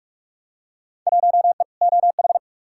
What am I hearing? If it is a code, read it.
Morse code, heard as a keyed tone, about 47 wpm, 700 Hz. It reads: 1E OH
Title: Field Day Exchanges - 45wpm